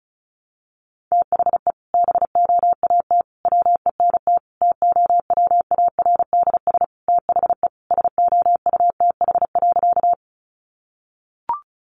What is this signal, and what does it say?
Morse code, heard as a keyed tone, about 34 wpm, 700 Hz. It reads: THE BOAT WENT TOWARDS THE SOUTH.
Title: Sentences from Top 1100 Words 35wpm